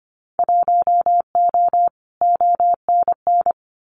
Morse code, heard as a keyed tone, about 25 wpm, 700 Hz. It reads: 1O ONN
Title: Winter Field Day Exchanges Encore 25wpm